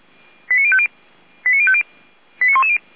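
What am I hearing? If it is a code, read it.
Five-tone selective call (ZVEI), heard as quick runs of five tones, tones of about 70 ms. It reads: 89050 89110
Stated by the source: Five-tone ZVEI selective call received on an SDR.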